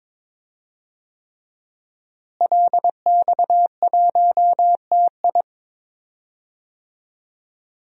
Morse code, heard as a keyed tone, about 22 wpm, 700 Hz. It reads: LX1TI
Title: DXCC-100 Call Signs - 5 Characters - Repeated 3x - 22wpm